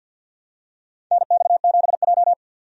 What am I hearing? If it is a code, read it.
Morse code, heard as a keyed tone, about 49 wpm, 700 Hz. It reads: NX7J
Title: USA Call Signs - 4 Characters - 50wpm